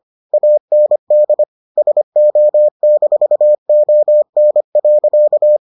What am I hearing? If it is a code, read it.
Morse code, heard as a keyed tone, about 25 wpm, 600 Hz. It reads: AND SO-ON.